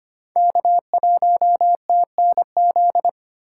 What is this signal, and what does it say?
Morse code, heard as a keyed tone, about 25 wpm, 700 Hz. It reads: K1TNZ
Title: USA Call Signs - 5 Characters - 4x Character Spacing - 25wpm